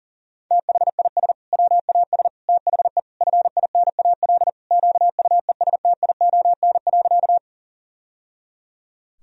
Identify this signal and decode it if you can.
Morse code, heard as a keyed tone, about 40 wpm, 700 Hz. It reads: THIS WAS THE FINAL QUESTION.
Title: Sentences from Top 1000 Words - Review Part 2 - 40wpm